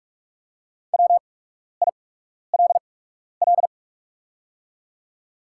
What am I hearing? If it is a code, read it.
Morse code, heard as a keyed tone, about 44 wpm, spaced as for 12 wpm, 690 Hz. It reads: WILL
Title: 8x Character Spacing - Top 200 Words - 45wpm